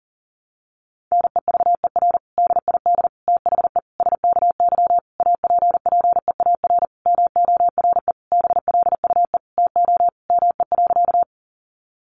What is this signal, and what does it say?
Morse code, heard as a keyed tone, about 40 wpm, 700 Hz. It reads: NEVER DID THE SKY APPEAR MORE BLUE TO ME.
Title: Sentences from Top 1200 Words 40wpm